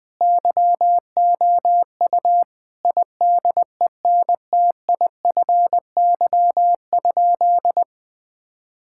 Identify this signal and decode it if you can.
Morse code, heard as a keyed tone, about 20 wpm, 700 Hz. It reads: YOU IDENTIFY?
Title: Sentences from Top 1000 Words - Review Part 6 - 20wpm